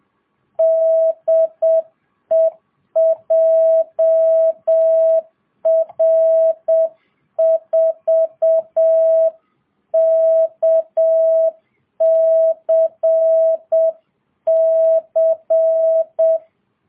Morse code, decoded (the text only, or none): DE JR4KCC